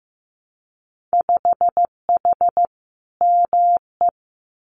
Morse code, heard as a keyed tone, about 15 wpm, 700 Hz. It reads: 5H ME
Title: Winter Field Day Exchanges Encore 15wpm